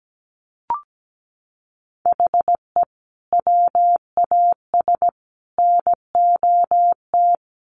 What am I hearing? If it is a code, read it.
Morse code, heard as a keyed tone, about 17 wpm, 700 Hz. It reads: HE WAS NOT
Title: Sets of 3 Words 17wpm